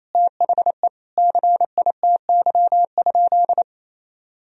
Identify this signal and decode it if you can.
Morse code, heard as a keyed tone, about 28 wpm, 700 Hz. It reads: THE CITY?
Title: Sentences from Top 1000 Words - Review Part 3 - 28wpm